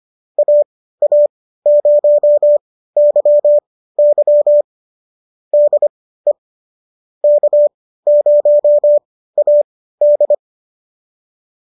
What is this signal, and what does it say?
Morse code, heard as a keyed tone, about 25 wpm, spaced as for 15 wpm, 570 Hz. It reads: AA0YY DE K0AD